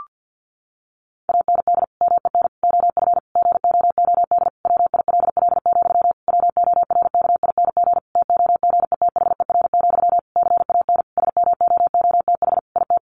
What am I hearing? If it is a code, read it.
Morse code, heard as a keyed tone, about 50 wpm, 700 Hz. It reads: AND MEN OF GOOD WILL, WORKING TOGETHER, CAN SMOOTH IT
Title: Sentences from Top 1400 Words 50wpm